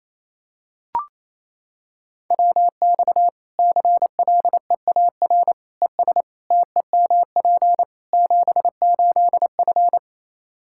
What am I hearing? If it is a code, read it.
Morse code, heard as a keyed tone, about 28 wpm, 700 Hz. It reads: WX CLEAR ES TEMP 78F
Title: Practice QSO - Exchange 2 - Info - 28wpm